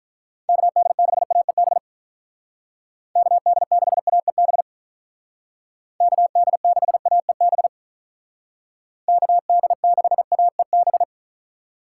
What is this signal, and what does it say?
Morse code, heard as a keyed tone, about 46 wpm, 700 Hz. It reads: KD6AEB KD6AEB KD6AEB KD6AEB